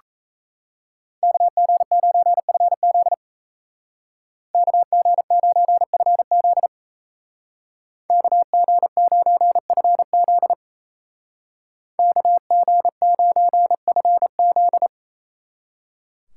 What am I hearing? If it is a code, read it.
Morse code, heard as a keyed tone, about 35 wpm, 700 Hz. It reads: KG9FZ KG9FZ KG9FZ KG9FZ
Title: Speed Racing - USA 2x2 Call Signs 28wpm